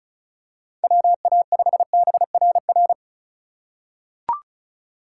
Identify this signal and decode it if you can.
Morse code, heard as a keyed tone, about 36 wpm, 700 Hz. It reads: WA5BRR
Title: USA Call Signs - 6 Characters - 2x Character Spacing - 35wpm